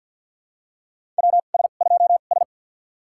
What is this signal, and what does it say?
Morse code, heard as a keyed tone, about 47 wpm, spaced as for 35 wpm, 700 Hz. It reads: WS2S
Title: USA Call Signs - 4 Characters - 2x Character Spacing - 50wpm